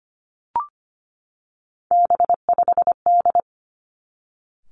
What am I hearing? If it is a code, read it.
Morse code, heard as a keyed tone, about 25 wpm, 700 Hz. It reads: B5D
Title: Three Challenging Letter-Number Combinations 25wpm